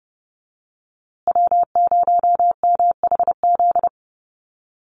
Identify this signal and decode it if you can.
Morse code, heard as a keyed tone, about 30 wpm, 700 Hz. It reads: W0MHZ
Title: USA Call Signs Repeated 3x 30wpm